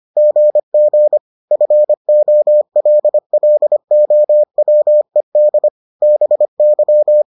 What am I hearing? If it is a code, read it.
Morse code, heard as a keyed tone, about 25 wpm, 600 Hz. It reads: GG FOLLOWED BY